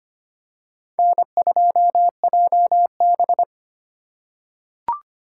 Morse code, heard as a keyed tone, about 25 wpm, 700 Hz. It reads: N2JB
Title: USA Call Signs - 4 Characters - 3x Character Spacing - 25wpm